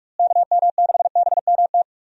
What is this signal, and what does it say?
Morse code, heard as a keyed tone, about 46 wpm, 700 Hz. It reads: KM6BMT
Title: USA Call Signs - 6 Characters - Repeated 3x - 45wpm